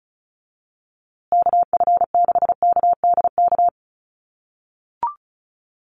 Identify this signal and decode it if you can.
Morse code, heard as a keyed tone, about 34 wpm, 700 Hz. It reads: KF6KDK
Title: USA Call Signs Repeated 3x 35wpm